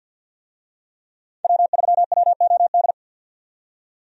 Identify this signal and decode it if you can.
Morse code, heard as a keyed tone, about 52 wpm, 700 Hz. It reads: W3WOD
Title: USA Call Signs - 5 Characters - 50wpm